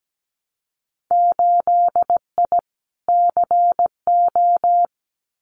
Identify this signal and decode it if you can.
Morse code, heard as a keyed tone, about 17 wpm, 700 Hz. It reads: 8I CO
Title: Winter Field Day Exchanges Encore 17wpm